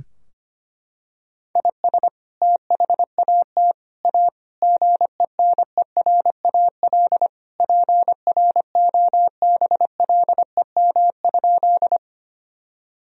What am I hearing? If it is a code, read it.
Morse code, heard as a keyed tone, about 25 wpm, 700 Hz. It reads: IS THAT A GENERAL PROBLEM?
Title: Sentences from Top 1000 Words - Review Part 3 - 25wpm